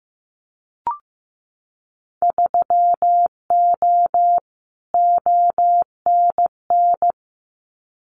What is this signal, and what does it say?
Morse code, heard as a keyed tone, about 15 wpm, 700 Hz. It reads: 3O ONN
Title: Winter Field Day Exchanges 15wpm